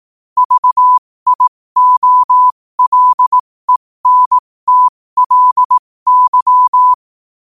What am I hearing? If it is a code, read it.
Morse code, heard as a keyed tone, about 18 wpm, spaced as for 15 wpm, 1000 Hz. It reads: VIOLENTLY
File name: qotdfiles / 15wpm